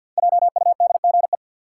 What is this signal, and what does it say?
Morse code, heard as a keyed tone, about 53 wpm, 700 Hz. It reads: JUDGE
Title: Five Letter Words - 50wpm